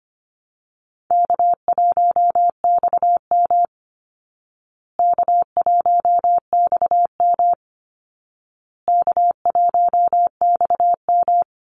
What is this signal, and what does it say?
Morse code, heard as a keyed tone, about 25 wpm, 700 Hz. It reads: K1XM K1XM K1XM
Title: Call Signs Repeated 2x - 25wpm